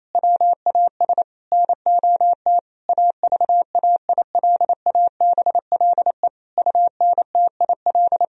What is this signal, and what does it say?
Morse code, heard as a keyed tone, about 27 wpm, 700 Hz. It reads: WAS NOT AVAILABLE UNTIL
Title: Sentences from Top 1000 Words - Review Part 3 - 28wpm